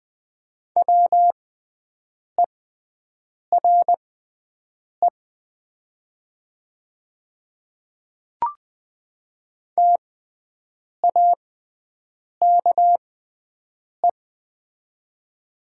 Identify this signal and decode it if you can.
Morse code, heard as a keyed tone, about 21 wpm, spaced as for 7 wpm, 700 Hz. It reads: WERE TAKE
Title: 6x Character Spacing - Top 200 Words - 20wpm